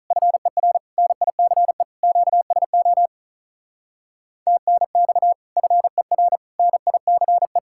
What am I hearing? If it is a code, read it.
Morse code, heard as a keyed tone, about 44 wpm, 700 Hz. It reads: FER NICE QSO TNX FER NICE
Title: Speed-Racing Common QSO Phrases - 35wpm